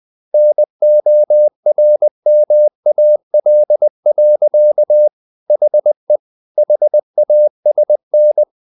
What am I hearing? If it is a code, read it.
Morse code, heard as a keyed tone, about 20 wpm, 600 Hz. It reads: NORMAL. HE HASN
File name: the old man and the sea / the old man and the sea part 1 20wpm